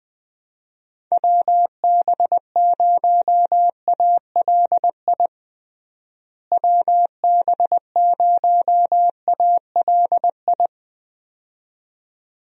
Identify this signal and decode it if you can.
Morse code, heard as a keyed tone, about 20 wpm, 700 Hz. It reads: WB0ALI WB0ALI
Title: USA Call Signs - 6 Characters - Repeated 2x - 20wpm